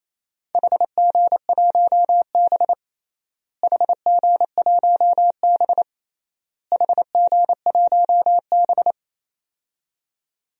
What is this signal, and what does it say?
Morse code, heard as a keyed tone, about 28 wpm, 700 Hz. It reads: HG1B HG1B HG1B
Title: DXCC-100 Call Signs - 4 Characters - Repeated 3x - 28wpm